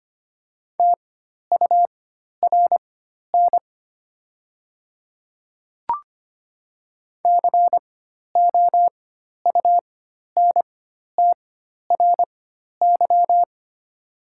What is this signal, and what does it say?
Morse code, heard as a keyed tone, about 25 wpm, spaced as for 12 wpm, 700 Hz. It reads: TURN COUNTRY
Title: 4x Character Spacing - Top 300 Words - 25wpm